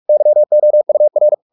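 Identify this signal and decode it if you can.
Morse code, heard as a keyed tone, about 44 wpm, 600 Hz. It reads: YOUR
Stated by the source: Morse practice at 45 wpm, the novel Never by Ken Follett in English, from CW med Gnister.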